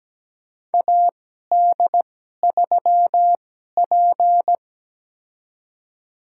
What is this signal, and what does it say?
Morse code, heard as a keyed tone, about 17 wpm, spaced as for 12 wpm, 700 Hz. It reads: AD3P
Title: USA Call Signs - 2x Character Spacing - 17wpm